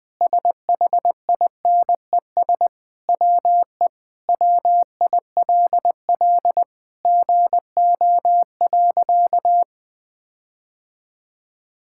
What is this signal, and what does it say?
Morse code, heard as a keyed tone, about 21 wpm, 700 Hz. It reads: SHINES WE WILL GO.
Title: Sentences with 4-Word Phrases - 20wpm